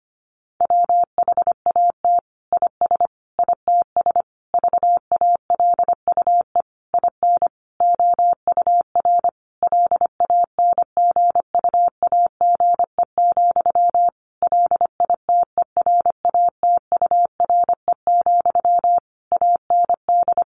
Morse code, heard as a keyed tone, about 25 wpm, 700 Hz. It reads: WHAT IS ITS VALUE IN OUR LANGUAGE, LITERATURE, AND
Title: Sentences from Top 600 Words 25wpm